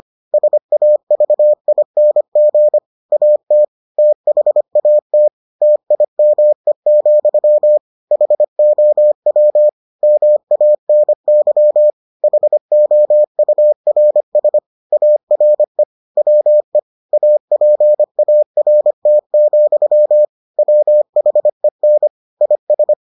Morse code, heard as a keyed tone, about 25 wpm, 600 Hz. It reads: SAVING AT THAT TIME, HOW MANY HOURS ARE WE APART, WHEN IS